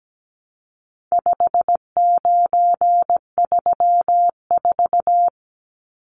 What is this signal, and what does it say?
Morse code, heard as a keyed tone, about 17 wpm, 700 Hz. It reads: 5934
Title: Numbers - 4-Digit 17wpm